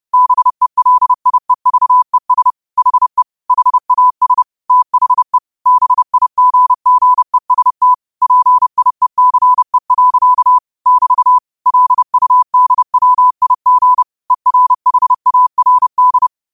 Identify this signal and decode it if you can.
Morse code, heard as a keyed tone, about 30 wpm, 1000 Hz. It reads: BELIEVES HE HAS THE BIGGEST PIECE. <BT> LUDWIG ERHARD
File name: qotdfiles / 30wpm